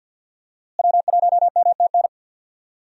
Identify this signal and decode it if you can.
Morse code, heard as a keyed tone, about 46 wpm, 700 Hz. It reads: W1MTN
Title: USA Call Signs - 5 Characters - Repeated 3x - 50wpm